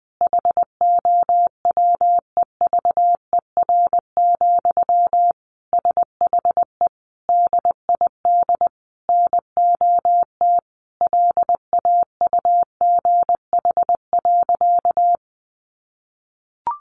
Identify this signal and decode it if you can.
Morse code, heard as a keyed tone, about 20 wpm, 700 Hz. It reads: HOWEVER, SHE DID NOT LAUGH.
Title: Sentences from Top 1100 Words 20wpm